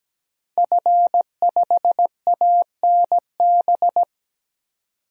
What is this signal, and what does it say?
Morse code, heard as a keyed tone, about 17 wpm, 700 Hz. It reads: F5ANB
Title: DXCC-100 Call Signs - 5 Characters - 17wpm